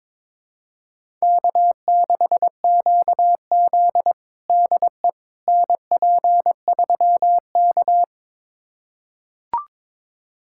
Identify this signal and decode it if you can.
Morse code, heard as a keyed tone, about 21 wpm, 700 Hz. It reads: K6QZ DE NP3K
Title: USA Callsign DE Callsign - 4 Characters - 22wpm